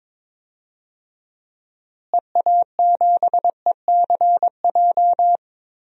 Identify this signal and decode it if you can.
Morse code, heard as a keyed tone, about 21 wpm, 700 Hz. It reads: EA7ECJ
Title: DXCC-100 Call Signs - 6 Characters - Repeated 3x - 22wpm